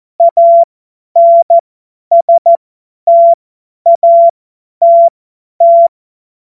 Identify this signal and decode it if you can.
Morse code, heard as a keyed tone, about 14 wpm, spaced as for 10 wpm, 680 Hz. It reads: ANSTATT
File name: Morsekurs / Witze / Witz 27. September 2022 50 F70